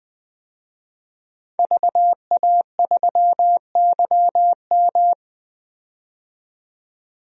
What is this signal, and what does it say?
Morse code, heard as a keyed tone, about 20 wpm, 700 Hz. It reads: VA3YM
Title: DXCC-100 Call Signs - 5 Characters - 20wpm